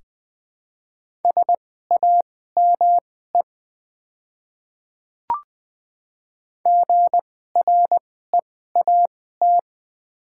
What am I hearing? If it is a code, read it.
Morse code, heard as a keyed tone, about 21 wpm, spaced as for 15 wpm, 700 Hz. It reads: SAME GREAT
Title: 2x Character Spacing - Top 300 Words - 20wpm